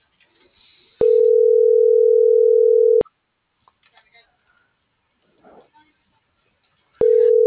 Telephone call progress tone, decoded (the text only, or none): ringback tone